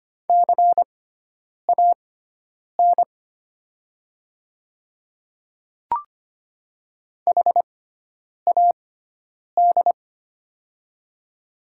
Morse code, heard as a keyed tone, about 25 wpm, spaced as for 9 wpm, 700 Hz. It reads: CAN HAD
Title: 6x Character Spacing - Top 100 Words - 25wpm